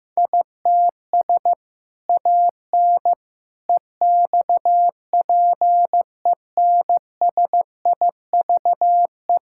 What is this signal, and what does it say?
Morse code, heard as a keyed tone, about 16 wpm, 700 Hz. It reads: ITS AN EXPENSIVE
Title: Sentences from Top 1000 Words - Review Part 6 - 15wpm